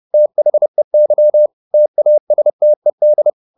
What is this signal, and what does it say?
Morse code, heard as a keyed tone, about 30 wpm, 600 Hz. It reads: THEY TASTED